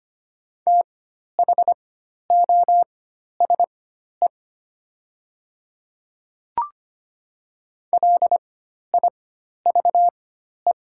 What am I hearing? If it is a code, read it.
Morse code, heard as a keyed tone, about 25 wpm, spaced as for 12 wpm, 700 Hz. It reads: THOSE LIVE
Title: 4x Character Spacing - Top 300 Words - 25wpm